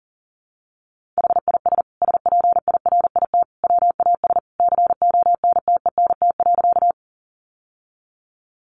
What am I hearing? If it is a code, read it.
Morse code, heard as a keyed tone, about 40 wpm, 700 Hz. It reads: HIS SPIRIT WAS CONTENT.
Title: Sentences from Top 1400 Words 40wpm